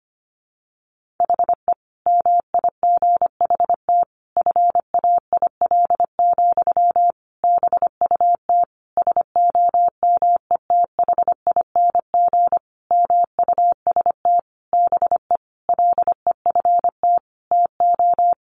Morse code, heard as a keyed tone, about 25 wpm, 700 Hz. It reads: HE MIGHT FAIL, BUT SOMETHING MUST BE LEFT TO
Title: Sentences from Top 1100 Words 25wpm